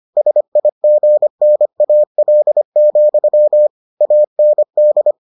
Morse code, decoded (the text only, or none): SIGNAL, AND